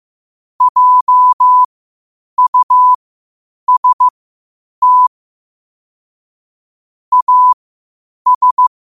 Morse code, heard as a keyed tone, about 15 wpm, spaced as for 8 wpm, 1000 Hz. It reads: JUST AS
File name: qotdfiles / 8wpm